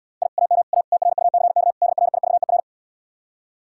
Morse code, heard as a keyed tone, about 80 wpm, 700 Hz. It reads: I AM A TRUCK DRIVER